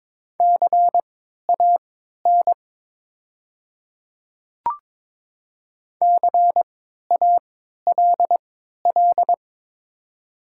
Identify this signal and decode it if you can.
Morse code, heard as a keyed tone, about 22 wpm, spaced as for 13 wpm, 700 Hz. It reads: CAN CALL
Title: 3x Character Spacing - Top 100 Words - 22wpm